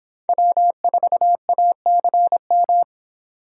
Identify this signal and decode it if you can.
Morse code, heard as a keyed tone, about 30 wpm, 700 Hz. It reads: W4ACM